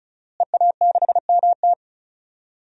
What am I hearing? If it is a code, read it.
Morse code, heard as a keyed tone, about 36 wpm, 700 Hz. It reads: EA6MT
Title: DXCC-100 Call Signs - 5 Characters - Repeated 2x - 35wpm